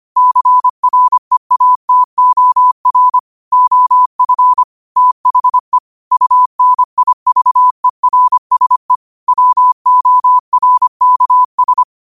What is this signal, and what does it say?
Morse code, heard as a keyed tone, about 25 wpm, 1000 Hz. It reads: CREATOR OF THE UNIVERSE WORKS
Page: New quotes every day in morse code at 25 Words per minute.